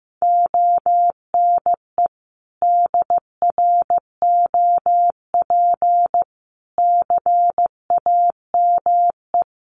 Morse code, heard as a keyed tone, about 15 wpm, 700 Hz. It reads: ONE DROP CAME
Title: Sentences from Top 1400 Words 15wpm